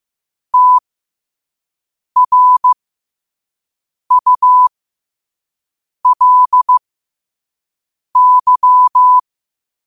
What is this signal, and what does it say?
Morse code, heard as a keyed tone, about 15 wpm, spaced as for 5 wpm, 1000 Hz. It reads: TRULY
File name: qotdfiles / 5wpm